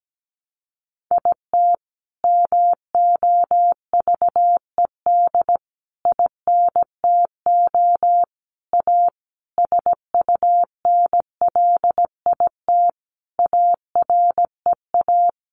Morse code, heard as a keyed tone, about 17 wpm, 700 Hz. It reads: IT MOVED INTO A SUNLIT AREA
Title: A Sound Of Thunder 17wpm